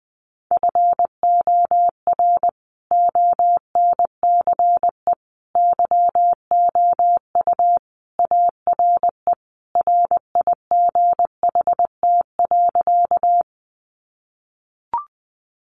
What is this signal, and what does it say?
Morse code, heard as a keyed tone, about 20 wpm, 700 Hz. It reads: FOR ONCE YOU ARE RIGHT.
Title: Sentences from Top 300 Words 20wpm